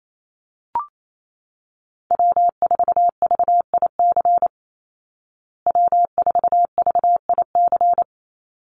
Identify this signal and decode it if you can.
Morse code, heard as a keyed tone, about 28 wpm, 700 Hz. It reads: W4VIC W4VIC
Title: Call Signs Repeated 2x - 28wpm